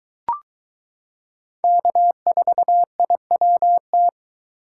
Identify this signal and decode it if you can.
Morse code, heard as a keyed tone, about 22 wpm, 700 Hz. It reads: K4IWT